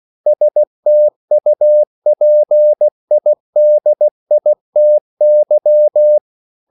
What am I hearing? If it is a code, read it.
Morse code, heard as a keyed tone, about 16 wpm, 600 Hz. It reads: STUPIDITY
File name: CW / Practice / EnglishQuotations0216